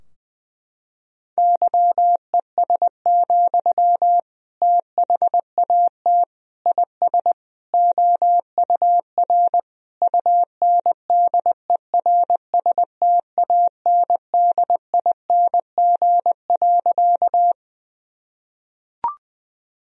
Morse code, heard as a keyed tone, about 20 wpm, 700 Hz. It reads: YES, THAT IS OUR UNDERSTANDING.
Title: Sentences from Top 1000 Words - Review Part 3 - 20wpm